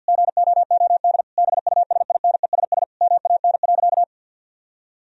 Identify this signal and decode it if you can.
Morse code, heard as a keyed tone, about 52 wpm, 700 Hz. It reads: GOOD BUSINESS MAN.
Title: Sentences from Top 1000 Words - Review Part 4 - 50wpm